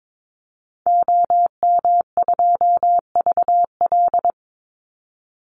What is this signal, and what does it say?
Morse code, heard as a keyed tone, about 22 wpm, 700 Hz. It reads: OM2VL
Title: Call Signs Repeated 2x - 22wpm